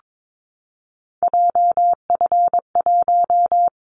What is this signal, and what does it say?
Morse code, heard as a keyed tone, about 22 wpm, 700 Hz. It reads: JF1
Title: Callsign Prefixes for Top 25 DXCC - 22wpm